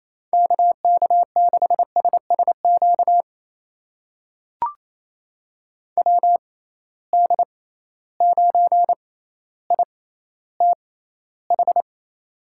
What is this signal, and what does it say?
Morse code, heard as a keyed tone, about 28 wpm, spaced as for 14 wpm, 700 Hz. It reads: KK6SSQ WD9ITH
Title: USA Call Signs - 6 Characters - 6x Character Spacing - 28wpm